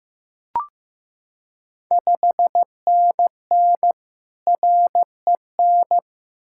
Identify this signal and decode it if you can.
Morse code, heard as a keyed tone, about 15 wpm, 700 Hz. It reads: 5NN REN
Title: Ontario QSO Party Exchange 15wpm